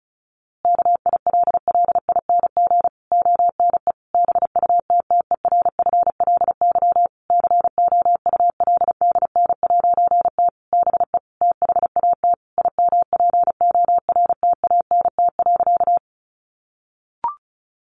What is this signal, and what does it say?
Morse code, heard as a keyed tone, about 34 wpm, 700 Hz. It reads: KILLING ONE BUTTERFLY COULDN'T BE THAT IMPORTANT.
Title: A Sound Of Thunder 35wpm